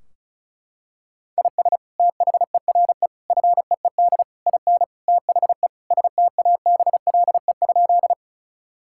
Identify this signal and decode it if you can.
Morse code, heard as a keyed tone, about 34 wpm, 700 Hz. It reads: IS THERE FEED IN THE STABLE?
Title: Sentences from Top 1600 Words 35wpm